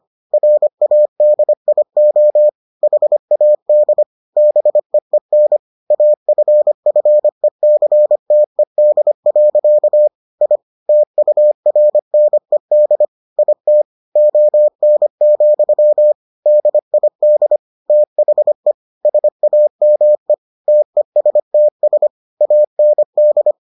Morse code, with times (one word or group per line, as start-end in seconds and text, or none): 0.3-2.5 RADIO
2.8-4.0 HAD
4.4-5.6 BEEN
5.9-10.1 AFFECTED.
10.4-10.6 I
10.9-13.0 TURNED
13.4-13.8 IT
14.2-16.1 ON,
16.5-17.6 DID
17.9-18.7 THE
19.0-20.3 SAME
20.7-22.1 TESTS
22.4-23.5 AND